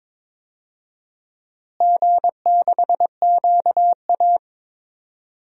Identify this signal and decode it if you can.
Morse code, heard as a keyed tone, about 21 wpm, 700 Hz. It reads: G6QA
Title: DXCC-100 Call Signs - 4 Characters - Repeated 3x - 22wpm